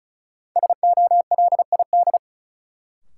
Morse code, heard as a keyed tone, about 36 wpm, 700 Hz. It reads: SOLID
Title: Sentences from Top 1500 Words - New Word Drill - 35wpm